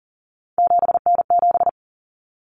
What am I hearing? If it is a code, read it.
Morse code, heard as a keyed tone, about 40 wpm, 700 Hz. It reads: 7N7
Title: Common Cut Numbers - 3-Digit 40wpm